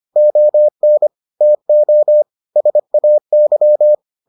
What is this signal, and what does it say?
Morse code, heard as a keyed tone, about 25 wpm, 600 Hz. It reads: ON TO SAY